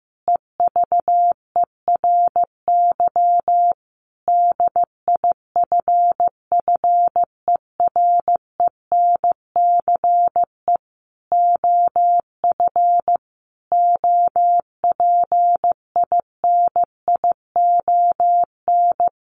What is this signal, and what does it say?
Morse code, heard as a keyed tone, about 14 wpm, 700 Hz. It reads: EVERY DIFFERENCE OF OPINION
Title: Sentences from Top 1400 Words 15wpm